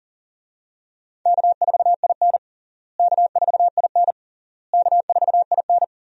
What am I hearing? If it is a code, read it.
Morse code, heard as a keyed tone, about 40 wpm, 700 Hz. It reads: K4IN K4IN K4IN
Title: USA Call Signs - 4 Characters - Repeated 3x - 40wpm